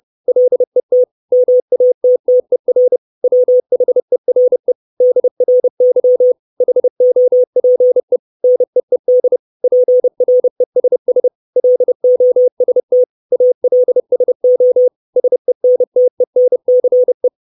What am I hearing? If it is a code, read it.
Morse code, heard as a keyed tone, about 30 wpm, 500 Hz. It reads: LET MATTER WHERE DRY HOPE NEED PRESS LOST ALSO SENTENCE
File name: 632 Words 30WPM